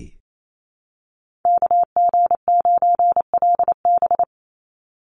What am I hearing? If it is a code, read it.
Morse code, heard as a keyed tone, about 28 wpm, 700 Hz. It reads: KG9LB
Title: Call Signs - 28wpm